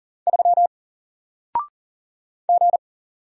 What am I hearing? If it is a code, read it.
Morse code, heard as a keyed tone, about 40 wpm, 700 Hz. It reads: 3 G
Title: Single Character (Letter:Number:Punctuation:Prosign) Alt 40wpm